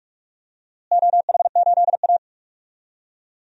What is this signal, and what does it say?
Morse code, heard as a keyed tone, about 47 wpm, 700 Hz. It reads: OH8A
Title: DXCC-100 Call Signs - 4 Characters - Repeated 2x - 45wpm